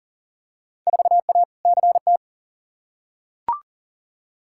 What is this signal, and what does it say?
Morse code, heard as a keyed tone, about 40 wpm, 700 Hz. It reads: 4A CT
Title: Field Day Exchanges Encore - 40wpm